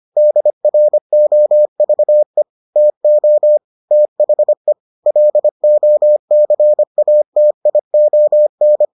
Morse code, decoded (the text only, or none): DROVE TO THE LOCATION